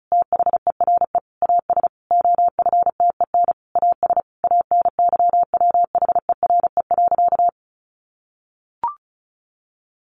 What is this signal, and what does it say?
Morse code, heard as a keyed tone, about 34 wpm, 700 Hz. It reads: THERE AS OFTEN AS ANYWHERE.
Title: Sentences from Top 1400 Words 35wpm